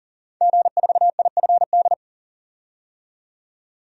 Morse code, heard as a keyed tone, about 40 wpm, 700 Hz. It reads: G4IFD
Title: DXCC-100 Call Signs - 5 Characters - 40wpm